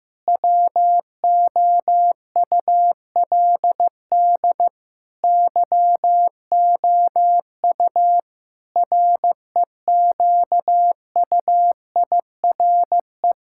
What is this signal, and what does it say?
Morse code, heard as a keyed tone, about 15 wpm, 700 Hz. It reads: WOULD YOU REQUIRE
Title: Sentences from Top 1000 Words - Review Part 6 - 15wpm